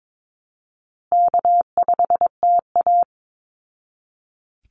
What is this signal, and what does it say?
Morse code, heard as a keyed tone, about 22 wpm, 700 Hz. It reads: K5TA
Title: Call Signs Repeated 2x - 22wpm